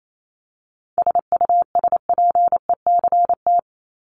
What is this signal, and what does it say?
Morse code, heard as a keyed tone, about 28 wpm, 700 Hz. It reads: SUSPECT
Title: Top 1751-2000 Words 28wpm